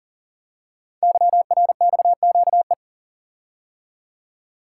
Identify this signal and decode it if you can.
Morse code, heard as a keyed tone, about 42 wpm, 690 Hz. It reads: YRXQE
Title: Code Groups - 5 Letters - 40wpm